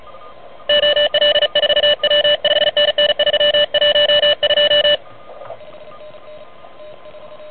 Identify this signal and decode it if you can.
Morse code, heard as a keyed tone, about 36 wpm, 2950 Hz, with noise. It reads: OL4W 5NN 312
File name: truSDX